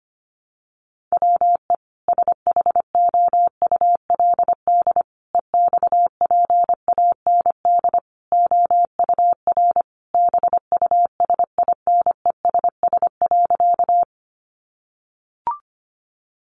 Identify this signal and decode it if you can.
Morse code, heard as a keyed tone, about 25 wpm, 700 Hz. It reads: WE SHOULD EXPAND OUR BUSINESS.
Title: Sentences from Top 1400 Words 25wpm